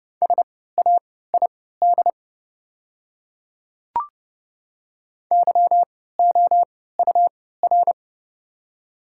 Morse code, heard as a keyed tone, about 32 wpm, spaced as for 17 wpm, 700 Hz. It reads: SAID YOUR
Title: 3x Character Spacing - Top 500 Words - 30wpm